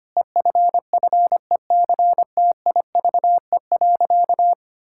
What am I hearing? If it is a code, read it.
Morse code, heard as a keyed tone, about 25 wpm, 700 Hz. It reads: EFFECTIVE.
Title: Sentences from Top 1000 Words - Review Part 6 - 25wpm